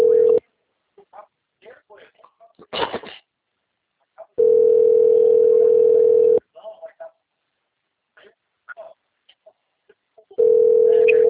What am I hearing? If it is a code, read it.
Ringback tone (North America), 440 + 480 Hz, with speech.
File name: out- - - - - .15766